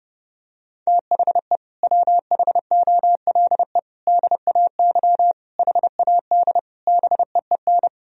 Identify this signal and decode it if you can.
Morse code, heard as a keyed tone, about 30 wpm, 700 Hz. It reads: THE WHOLE DAY HAD BEEN
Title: Sentences from Top 1000 Words - Review Part 3 - 30wpm